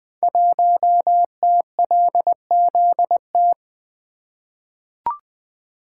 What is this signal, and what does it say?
Morse code, heard as a keyed tone, about 20 wpm, 700 Hz. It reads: 1TLZT
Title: Code Groups - 5 Letter:Numbers - 20wpm